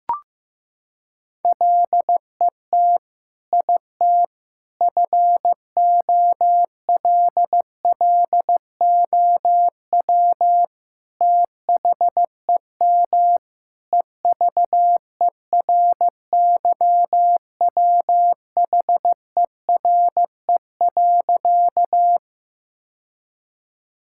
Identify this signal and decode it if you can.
Morse code, heard as a keyed tone, about 15 wpm, 700 Hz. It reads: LET IT FOLLOW THEM EVERYWHERE.
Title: Sentences from Top 1600 Words 15wpm